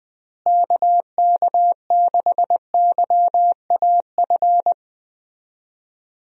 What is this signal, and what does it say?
Morse code, heard as a keyed tone, about 20 wpm, 700 Hz. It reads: KK6YAF
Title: USA Call Signs - 6 Characters - Repeated 3x - 20wpm